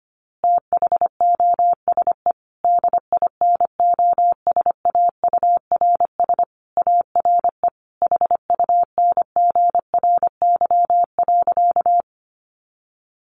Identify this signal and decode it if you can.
Morse code, heard as a keyed tone, about 25 wpm, 700 Hz. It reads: THOSE DINOSAURS ARE HUNGRY.
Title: A Sound Of Thunder 25wpm